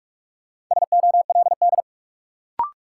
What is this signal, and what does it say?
Morse code, heard as a keyed tone, about 46 wpm, 700 Hz. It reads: SOLD
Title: Sentences from Top 1400 Words - New Word Drill Alt - 45wpm